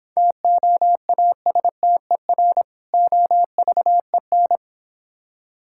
Morse code, heard as a keyed tone, about 26 wpm, 700 Hz. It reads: TOASTER OVEN